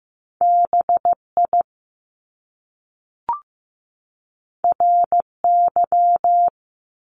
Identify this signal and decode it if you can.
Morse code, heard as a keyed tone, about 15 wpm, 700 Hz. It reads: BI RY
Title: Top Two Letter Combinations in Words 15wpm